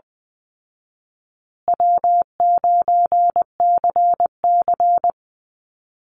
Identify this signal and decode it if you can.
Morse code, heard as a keyed tone, about 20 wpm, 700 Hz. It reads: W9CC
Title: Easy Call Signs Repeated 3x - 20wpm